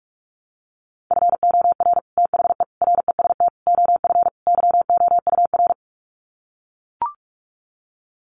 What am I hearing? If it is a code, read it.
Morse code, heard as a keyed tone, about 44 wpm, 700 Hz. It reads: FOR THE REST OF YOUR
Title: Sets of 5 Words Encore 45wpm